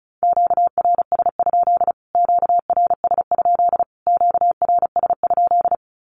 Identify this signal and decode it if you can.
Morse code, heard as a keyed tone, about 34 wpm, 700 Hz. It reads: QRS? QRS? QRS?
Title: Q-Codes Repeated 3x 35wpm